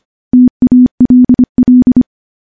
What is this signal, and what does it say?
Morse code, heard as a keyed tone, about 25 wpm, 250 Hz. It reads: TALL
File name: morse / morse-training-25WPM